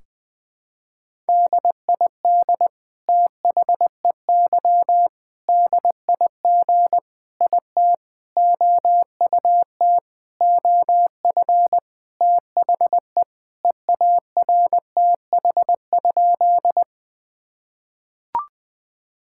Morse code, heard as a keyed tone, about 20 wpm, 700 Hz. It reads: DID THEY DIG IT OUT OF THE EARTH?
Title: Sentences from Top 1600 Words 20wpm